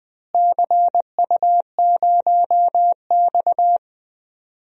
Morse code, heard as a keyed tone, about 20 wpm, 700 Hz. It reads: CU0X
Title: DXCC-100 Call Signs - 4 Characters - Repeated 2x - 20wpm